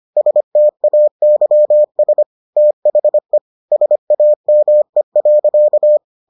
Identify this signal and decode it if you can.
Morse code, heard as a keyed tone, about 25 wpm, 600 Hz. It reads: STAYS THE SAME.